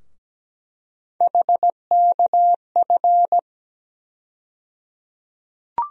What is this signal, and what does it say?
Morse code, heard as a keyed tone, about 17 wpm, 700 Hz. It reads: HKF
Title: USA Call Sign Postfixes - 3 Letters 17wpm